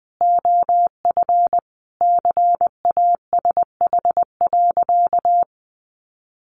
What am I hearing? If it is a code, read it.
Morse code, heard as a keyed tone, about 20 wpm, 700 Hz. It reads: OF CASH.
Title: Sentences from Top 1300 Words 20wpm